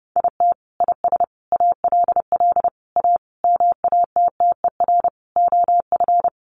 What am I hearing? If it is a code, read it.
Morse code, heard as a keyed tone, about 30 wpm, 700 Hz. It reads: IT IS ALL A MATTER OF
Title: Sentences from Top 1300 Words 30wpm